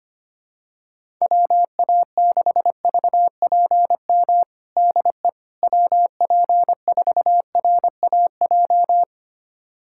Morse code, heard as a keyed tone, about 25 wpm, 700 Hz. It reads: WA6VPM DE WP4RAJ
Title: USA Callsign DE Callsign - 6 Characters - 25wpm